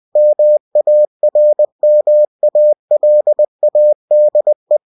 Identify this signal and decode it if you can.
Morse code, heard as a keyed tone, about 20 wpm, 600 Hz. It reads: MARMALADE